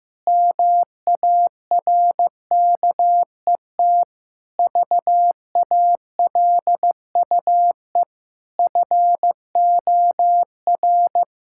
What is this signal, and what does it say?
Morse code, heard as a keyed tone, about 15 wpm, 700 Hz. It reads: MARKET VALUE FOR
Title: Sentences from Top 1000 Words - Review Part 6 - 15wpm